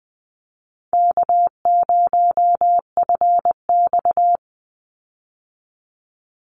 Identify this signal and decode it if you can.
Morse code, heard as a keyed tone, about 20 wpm, 700 Hz. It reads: K0FX
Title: Easy Call Signs Repeated 3x - 20wpm